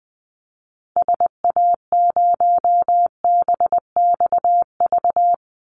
Morse code, heard as a keyed tone, about 20 wpm, 700 Hz. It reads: SA0BXV
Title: Call Signs Repeated 3x - 20wpm